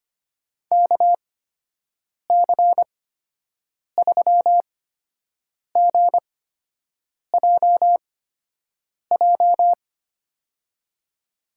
Morse code, heard as a keyed tone, about 25 wpm, spaced as for 7 wpm, 700 Hz. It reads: KC3GJJ
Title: USA Call Signs - 6 Characters - 8x Character Spacing - 25wpm